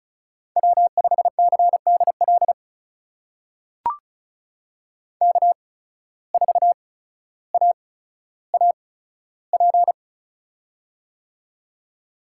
Morse code, heard as a keyed tone, about 36 wpm, spaced as for 15 wpm, 700 Hz. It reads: W5CDL K4AAP
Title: USA Call Signs - 5 Characters - 8x Character Spacing - 35wpm